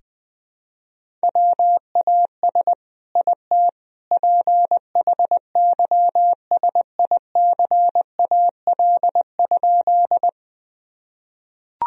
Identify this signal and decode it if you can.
Morse code, heard as a keyed tone, about 20 wpm, 700 Hz. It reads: WAS IT PHYSICAL?
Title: Sentences from Top 1000 Words - Review Part 4 - 20wpm